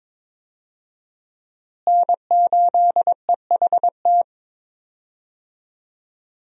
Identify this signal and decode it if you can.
Morse code, heard as a keyed tone, about 22 wpm, 700 Hz. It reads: N8EHT
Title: USA Call Signs - 5 Characters - Repeated 3x - 22wpm